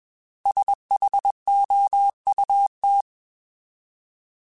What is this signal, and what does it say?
Morse code, heard as a keyed tone, about 21 wpm, 800 Hz. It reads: SHOUT
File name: Alphabetical / Alphabetical mp3s / S vs H @20